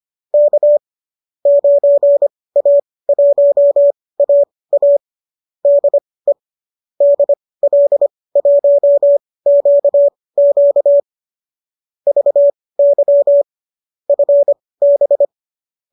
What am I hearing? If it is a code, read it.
Morse code, heard as a keyed tone, about 25 wpm, spaced as for 18 wpm, 590 Hz. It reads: K 9A1AA DE DL1QQ VY FB